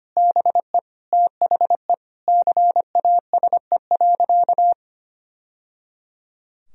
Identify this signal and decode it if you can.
Morse code, heard as a keyed tone, about 25 wpm, 700 Hz. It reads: BE THE CASE.
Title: Sentences from Top 1000 Words - Review Part 2 - 25wpm